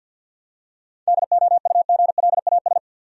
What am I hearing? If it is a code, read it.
Morse code, heard as a keyed tone, about 51 wpm, 700 Hz. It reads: DOUGLAS
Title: USA Common Names - 50wpm